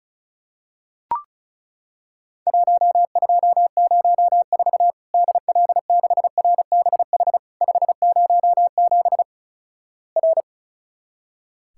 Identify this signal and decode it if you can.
Morse code, heard as a keyed tone, about 36 wpm, 700 Hz. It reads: 1204 DL6RBH 507 R
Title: Worked All Europe DX Contest - QTC Exchange - 35wpm